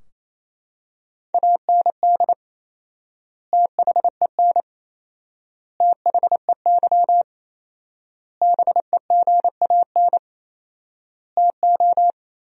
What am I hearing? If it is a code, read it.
Morse code, heard as a keyed tone, about 27 wpm, 700 Hz. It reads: AND THEN THEY BEGAN TO
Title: 4x Word-Spacing - Sentences from Top 300 Words - 28wpm